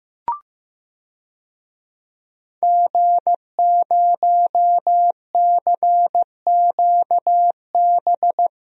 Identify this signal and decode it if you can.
Morse code, heard as a keyed tone, about 15 wpm, 700 Hz. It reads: G0CQB
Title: DXCC-100 Call Signs - 5 Characters - Repeated 2x - 15wpm